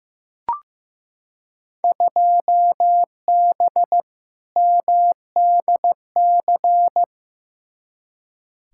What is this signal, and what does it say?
Morse code, heard as a keyed tone, about 15 wpm, 700 Hz. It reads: 2B MDC
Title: Field Day Exchanges - 15wpm